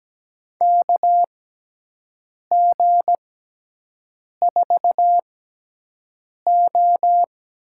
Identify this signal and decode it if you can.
Morse code, heard as a keyed tone, about 17 wpm, spaced as for 6 wpm, 700 Hz. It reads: KG4O
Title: USA Call Signs - 6 Characters - 6x Character Spacing - 17wpm